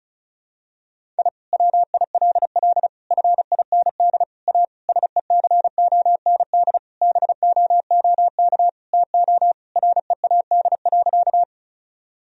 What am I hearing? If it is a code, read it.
Morse code, heard as a keyed tone, about 34 wpm, 700 Hz. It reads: I WILL FIND A SECOND BOOK TO READ.
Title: Sentences from Top 1000 Words - Review Part 1 - 35wpm